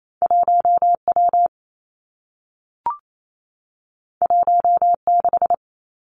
Morse code, heard as a keyed tone, about 28 wpm, 700 Hz. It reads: 1 W 1 6
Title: Two Challenging Letter-Number Combinations 28wpm